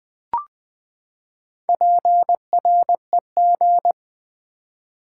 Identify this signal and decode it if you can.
Morse code, heard as a keyed tone, about 20 wpm, 700 Hz. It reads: PREG
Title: Build A Word - 4 Letters At A Time - 8-Letter Words - 20wpm